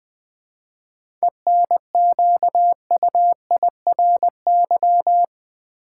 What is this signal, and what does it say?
Morse code, heard as a keyed tone, about 20 wpm, 700 Hz. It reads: ENQUIRY
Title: 7 Letter Words - First 3 Letters Given - 20wpm